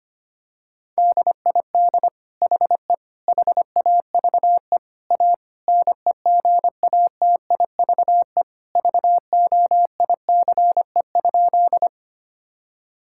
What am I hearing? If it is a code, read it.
Morse code, heard as a keyed tone, about 25 wpm, 700 Hz. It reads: DID HE HAVE A NEGATIVE VOICE?
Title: Sentences from Top 1000 Words - Review Part 6 - 25wpm